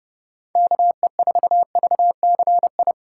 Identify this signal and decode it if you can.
Morse code, heard as a keyed tone, about 28 wpm, 700 Hz. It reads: KE4VCI
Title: USA Callsign DE Callsign - 6 Characters - 30wpm